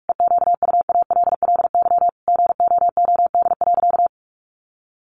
Morse code, heard as a keyed tone, about 44 wpm, 700 Hz. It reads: EQUALLY GOOD.
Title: Sentences from Top 1400 Words 45wpm